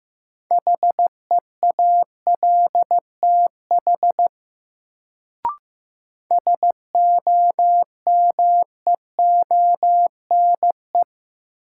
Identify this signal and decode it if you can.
Morse code, heard as a keyed tone, about 16 wpm, 700 Hz. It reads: HEALTH SOMEONE
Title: Sentences from Top 600 Words - New Word Drill - 15wpm